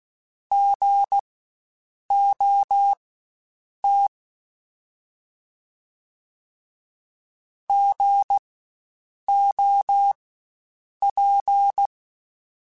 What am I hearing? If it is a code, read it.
Morse code, heard as a keyed tone, about 16 wpm, spaced as for 7 wpm, 800 Hz. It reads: GOT GOP
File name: Spaced Alphas / Spaced Alphas mp3s / Gs sp @15